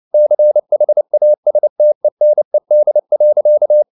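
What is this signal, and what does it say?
Morse code, heard as a keyed tone, about 29 wpm, 600 Hz. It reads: CHASTENED.
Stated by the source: Morse code at 29 wpm, sending Ken Follett's novel Never in English, from CW med Gnister.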